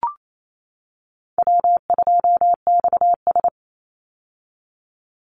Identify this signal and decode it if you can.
Morse code, heard as a keyed tone, about 28 wpm, 700 Hz. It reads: W2XS
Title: Easy Call Signs - 28wpm